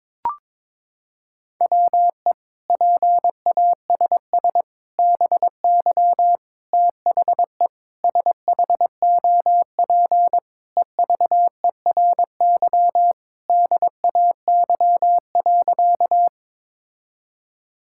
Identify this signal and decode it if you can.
Morse code, heard as a keyed tone, about 22 wpm, 700 Hz. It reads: WE PASS BY THE SHOP EVERY DAY.
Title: Sentences from Top 1600 Words 22wpm